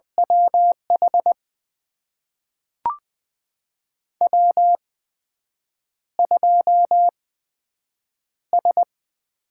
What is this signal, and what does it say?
Morse code, heard as a keyed tone, about 20 wpm, spaced as for 7 wpm, 700 Hz. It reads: WH W2S
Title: USA Call Signs - 8x Character Spacing - 20wpm